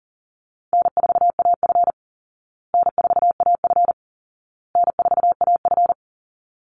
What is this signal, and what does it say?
Morse code, heard as a keyed tone, about 40 wpm, 700 Hz. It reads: N4AF N4AF N4AF
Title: Call Signs Repeated 2x - 40wpm